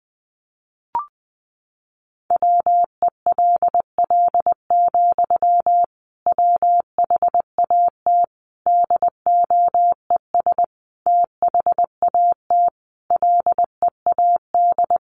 Morse code, heard as a keyed tone, about 20 wpm, 700 Hz. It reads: WELL, WHAT DOES THAT LEAD
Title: Sentences from Top 500 Words 20wpm